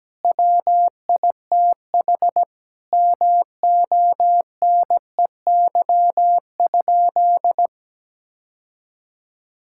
Morse code, heard as a keyed tone, about 17 wpm, 700 Hz. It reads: WITH MONEY?
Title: Sentences from Top 1600 Words 17wpm